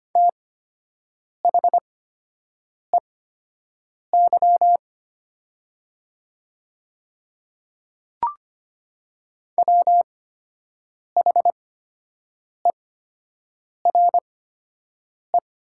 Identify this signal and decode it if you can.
Morse code, heard as a keyed tone, about 25 wpm, spaced as for 7 wpm, 700 Hz. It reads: THEY WHERE
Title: 8x Character Spacing - Top 200 Words - 25wpm